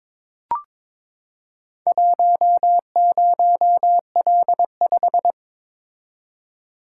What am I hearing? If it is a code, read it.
Morse code, heard as a keyed tone, about 22 wpm, 700 Hz. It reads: 10L5
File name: practice / 22wpm / Four Letter-Number Combinations - 22wpm